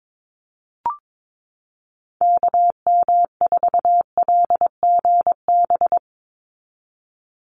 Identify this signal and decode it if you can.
Morse code, heard as a keyed tone, about 22 wpm, 700 Hz. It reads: KM4LGB
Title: USA Call Signs 22wpm